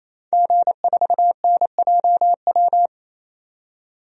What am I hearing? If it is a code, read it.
Morse code, heard as a keyed tone, about 27 wpm, 700 Hz. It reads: G4NJW
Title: DXCC-100 Call Signs - 5 Characters - 28wpm